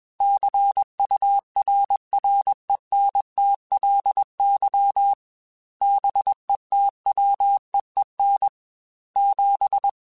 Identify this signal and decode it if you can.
Morse code, heard as a keyed tone, about 21 wpm, 800 Hz. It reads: CURRENTLY BETWEEN 7
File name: Weather Sets / Weather Sets-mp3s / Weather 5 22 @20